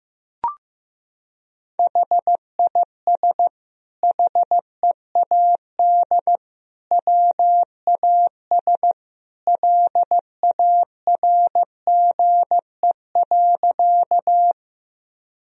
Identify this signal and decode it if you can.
Morse code, heard as a keyed tone, about 15 wpm, 700 Hz. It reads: HIS HEAD WAS LARGE.
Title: Sentences from Top 1000 Words - Review Part 1 - 15wpm